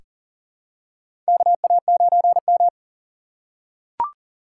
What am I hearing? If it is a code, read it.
Morse code, heard as a keyed tone, about 38 wpm, 700 Hz. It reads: KA9M
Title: USA Call Signs - 4 Characters - 4x Character Spacing - 40wpm